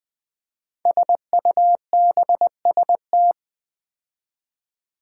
Morse code, heard as a keyed tone, about 20 wpm, 700 Hz. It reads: SUBST
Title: Build A Word - 5 Letters At A Time - 12-Letter Words - 20wpm